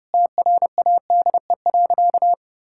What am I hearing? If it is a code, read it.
Morse code, heard as a keyed tone, about 30 wpm, 700 Hz. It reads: TRADE.
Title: Sentences from Top 1000 Words - Review Part 5 - 30wpm